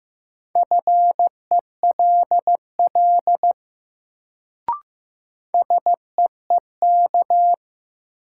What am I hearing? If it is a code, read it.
Morse code, heard as a keyed tone, about 15 wpm, 700 Hz. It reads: FELL SEEK
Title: Sentences from Top 1600 Words - New Word Drill Alt - 15wpm